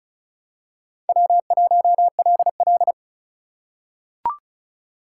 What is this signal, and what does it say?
Morse code, heard as a keyed tone, about 35 wpm, 700 Hz. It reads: W1LL
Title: USA Call Signs - 4 Characters - 4x Character Spacing - 35wpm